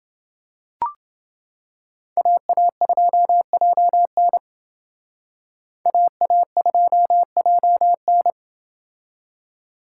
Morse code, heard as a keyed tone, about 29 wpm, 700 Hz. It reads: AA2JN AA2JN